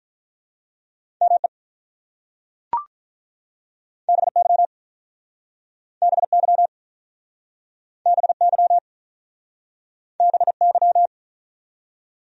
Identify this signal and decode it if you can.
Morse code, heard as a keyed tone, about 43 wpm, 700 Hz. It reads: ME BY BY BY BY
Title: Speed-Racing Top 200 Words 35wpm